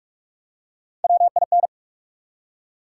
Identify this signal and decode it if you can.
Morse code, heard as a keyed tone, about 44 wpm, 700 Hz. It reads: WIN
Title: Three Letter Words - 45wpm